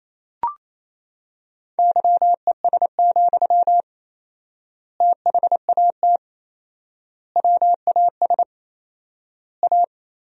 Morse code, heard as a keyed tone, about 28 wpm, 700 Hz. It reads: YES, THAT WAS A
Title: 4x Word-Spacing - Sentences from Top 500 Words - 28wpm